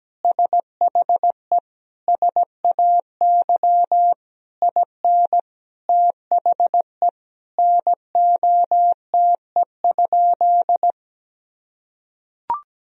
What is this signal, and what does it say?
Morse code, heard as a keyed tone, about 17 wpm, 700 Hz. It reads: SHE SAY IN THE NOTE?
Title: Sentences from Top 1000 Words - Review Part 2 - 17wpm